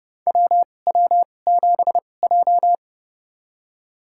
Morse code, heard as a keyed tone, about 30 wpm, spaced as for 22 wpm, 700 Hz. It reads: WW7J
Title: USA Call Signs - 4 Characters - 2x Character Spacing - 30wpm